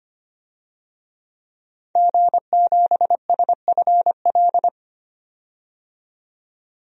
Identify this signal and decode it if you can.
Morse code, heard as a keyed tone, about 25 wpm, 700 Hz. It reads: G7SFL
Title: DXCC-100 Call Signs - 5 Characters - 25wpm